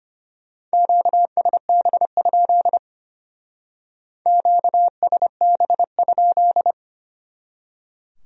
Morse code, heard as a keyed tone, about 33 wpm, 700 Hz. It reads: QSB? QSB?